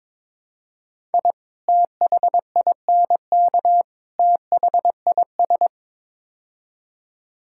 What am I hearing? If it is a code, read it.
Morse code, heard as a keyed tone, about 21 wpm, 700 Hz. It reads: I THINK THIS
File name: practice / 22wpm / Build A Phrase - 5-Word Phrases - 22wpm